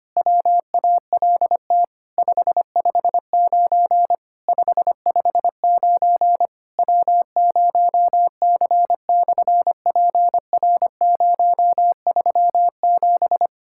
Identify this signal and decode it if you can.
Morse code, heard as a keyed tone, about 25 wpm, 700 Hz. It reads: WALT 559 559 W0C/PR037
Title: SOTA QSO Practice 25wpm